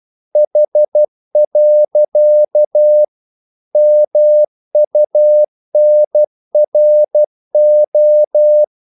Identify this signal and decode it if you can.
Morse code, heard as a keyed tone, about 12 wpm, 600 Hz. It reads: H. MUNRO
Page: English quotations in CW (telegraphy - morse)